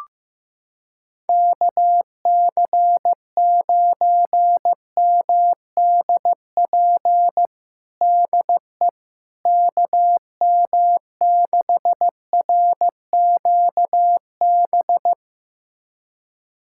Morse code, heard as a keyed tone, about 15 wpm, 700 Hz. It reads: KC9MDP DE KM6RQB
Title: USA Callsign DE Callsign - 6 Characters - 15wpm